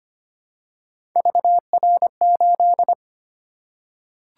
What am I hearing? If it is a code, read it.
Morse code, heard as a keyed tone, about 25 wpm, 700 Hz. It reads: VR8
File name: Three Letter-Number Combinations - 25wpm